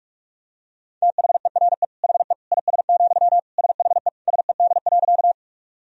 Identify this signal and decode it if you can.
Morse code, heard as a keyed tone, about 46 wpm, 700 Hz. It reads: THERE HE IS, SHE SED.
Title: Sentences from Top 100 Words and Abbreviations 45wpm